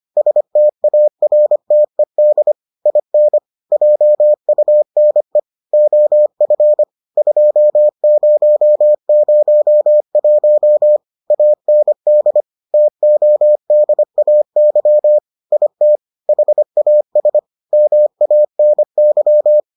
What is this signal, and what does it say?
Morse code, heard as a keyed tone, about 25 wpm, 600 Hz. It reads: STARTED IN JUNE OF 2001 AND TODAY IT HAS MANY